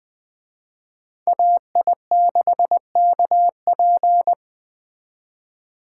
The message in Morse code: AI6KP

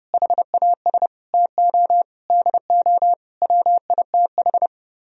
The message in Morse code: HAS TO DO WITH

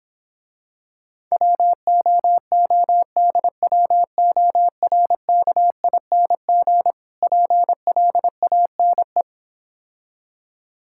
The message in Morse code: WOODWORKING PLANE

